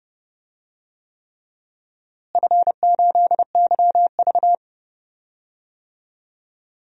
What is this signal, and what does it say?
Morse code, heard as a keyed tone, about 30 wpm, 700 Hz. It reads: F8YV
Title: DXCC-100 Call Signs - 4 Characters - Repeated 3x - 30wpm